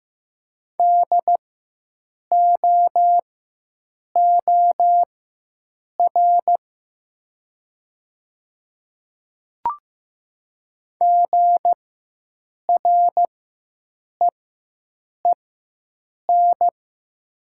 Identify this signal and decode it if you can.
Morse code, heard as a keyed tone, about 15 wpm, spaced as for 7 wpm, 700 Hz. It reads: DOOR GREEN
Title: 4x Character Spacing - Top 500 Words - 15wpm